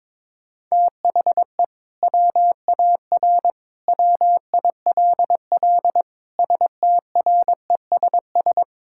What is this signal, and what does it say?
Morse code, heard as a keyed tone, about 22 wpm, 700 Hz. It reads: THE WAR WILL STRESS
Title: Sentences from Top 1000 Words - Review Part 6 - 22wpm